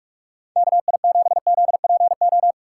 Morse code, heard as a keyed tone, about 44 wpm, 700 Hz. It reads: KI7ZPO